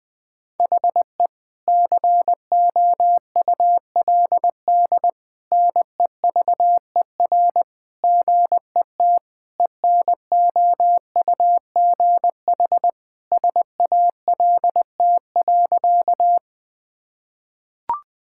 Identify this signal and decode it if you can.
Morse code, heard as a keyed tone, about 19 wpm, 700 Hz. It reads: HE COULD NEVER GET ENOUGH SALT.
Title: Sentences from Top 1000 Words - Review Part 5 - 20wpm